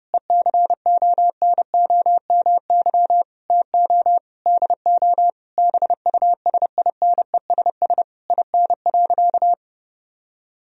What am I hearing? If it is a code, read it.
Morse code, heard as a keyed tone, about 32 wpm, 700 Hz. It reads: ECONOMY TO DO BUSINESS IN.
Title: Sentences from Top 1000 Words - Review Part 3 - 30wpm